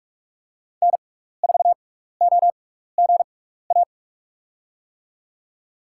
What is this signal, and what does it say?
Morse code, heard as a keyed tone, about 47 wpm, spaced as for 16 wpm, 700 Hz. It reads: N4OGA
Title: USA Call Signs - 5 Characters - 6x Character Spacing - 45wpm